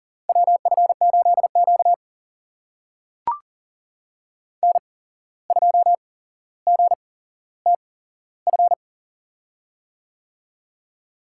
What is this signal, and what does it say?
Morse code, heard as a keyed tone, about 40 wpm, spaced as for 16 wpm, 700 Hz. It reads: WF8Q N2GTF
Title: USA Call Signs - 8x Character Spacing - 40wpm